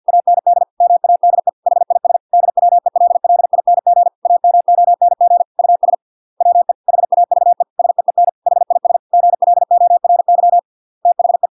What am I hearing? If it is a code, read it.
Morse code, heard as a keyed tone, about 50 wpm, 700 Hz. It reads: AND MADE HIS DWELLING AMONG US WE HAVE SEEN HIS GLORY THE